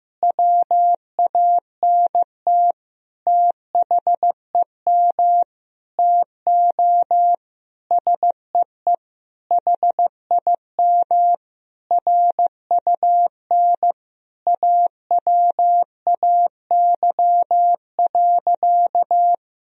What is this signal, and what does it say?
Morse code, heard as a keyed tone, about 16 wpm, 700 Hz. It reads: WANT THEM TO SEE HIM RUN AWAY.
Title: Sentences from Top 1000 Words - Review Part 1 - 15wpm